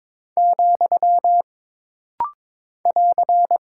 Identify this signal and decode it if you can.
Morse code, heard as a keyed tone, about 22 wpm, 700 Hz. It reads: , <AR>
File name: practice / 22wpm / Single Punctuation-Prosign - Rapid-Fire - 22wpm